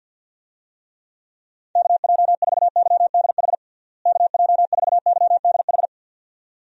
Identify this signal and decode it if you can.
Morse code, heard as a keyed tone, about 52 wpm, 700 Hz. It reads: KJ4YDH KJ4YDH
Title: USA Call Signs - 6 Characters - Repeated 2x - 50wpm